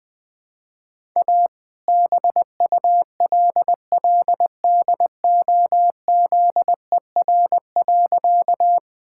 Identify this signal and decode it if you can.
Morse code, heard as a keyed tone, about 20 wpm, 700 Hz. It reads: A BULLDOZER.
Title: All-Age Appropriate Jokes - 20wpm